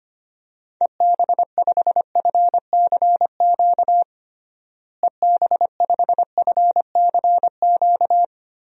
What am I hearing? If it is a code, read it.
Morse code, heard as a keyed tone, about 25 wpm, 700 Hz. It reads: EB5FCQ EB5FCQ
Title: DXCC-100 Call Signs - 6 Characters - Repeated 2x - 25wpm